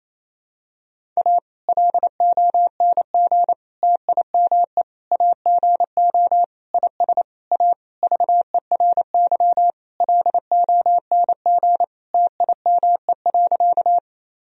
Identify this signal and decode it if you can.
Morse code, heard as a keyed tone, about 28 wpm, 700 Hz. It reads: A LONG TIME AGO IS A VERY LONG TIME.
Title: Sentences from Top 1000 Words - Review Part 3 - 28wpm